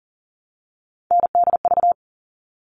Morse code, heard as a keyed tone, about 40 wpm, 700 Hz. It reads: NDV
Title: All Cut Numbers - 3-Digit 40wpm